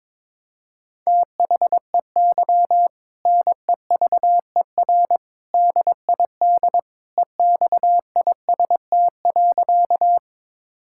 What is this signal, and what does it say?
Morse code, heard as a keyed tone, about 21 wpm, 700 Hz. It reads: THEY NEVER DID EXIST.
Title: Sentences from Top 1000 Words - Review Part 5 - 22wpm